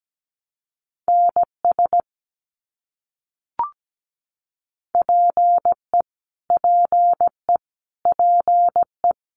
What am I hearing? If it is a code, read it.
Morse code, heard as a keyed tone, about 17 wpm, 700 Hz. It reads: NS PE PE PE
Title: Canadian Provinces and Territories - Repeated 3x - 17wpm